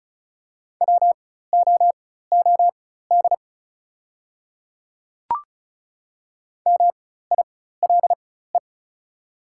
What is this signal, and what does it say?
Morse code, heard as a keyed tone, about 34 wpm, spaced as for 16 wpm, 700 Hz. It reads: WOOD MILE
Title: 4x Character Spacing - Top 300 Words - 35wpm